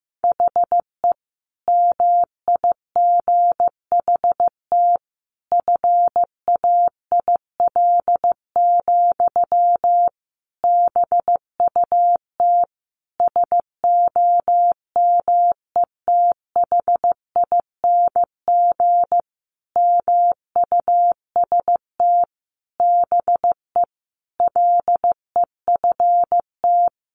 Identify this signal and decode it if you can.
Morse code, heard as a keyed tone, about 15 wpm, 700 Hz. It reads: HE MIGHT FAIL, BUT SOMETHING MUST BE LEFT
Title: Sentences from Top 1100 Words 15wpm